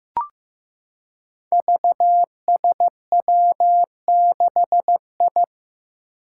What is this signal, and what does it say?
Morse code, heard as a keyed tone, about 15 wpm, 700 Hz. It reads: VSW6I
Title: Five Challenging Letter-Number Combinations 15wpm